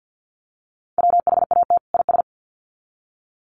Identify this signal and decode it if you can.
Morse code, heard as a keyed tone, about 48 wpm, 700 Hz. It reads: WHAT IS
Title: Sets of 2 Words 50wpm